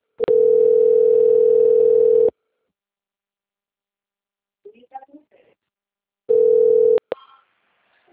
Ringback tone (North America), 440 + 480 Hz.
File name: rg-1600-unknown- - - .15990